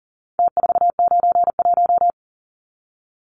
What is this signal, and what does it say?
Morse code, heard as a keyed tone, about 40 wpm, 700 Hz. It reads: T491
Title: Numbers - Contest Style - 4-Digit 40wpm